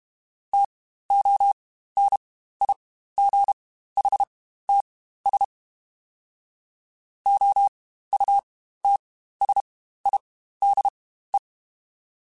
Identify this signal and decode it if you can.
Morse code, heard as a keyed tone, about 32 wpm, spaced as for 15 wpm, 800 Hz. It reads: TONIGHTS OUTSIDE
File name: Spaced Characters / Spaced Characters-mp3s / Weather 4 21 sp @30